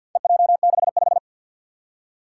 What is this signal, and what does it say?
Morse code, heard as a keyed tone, about 51 wpm, 700 Hz. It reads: EJ65
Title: Four Challenging Letter-Number Combinations 50wpm